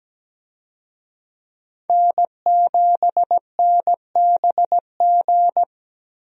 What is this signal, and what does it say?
Morse code, heard as a keyed tone, about 17 wpm, 700 Hz. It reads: N7NBG
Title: USA Call Signs - 5 Characters - Repeated 2x - 17wpm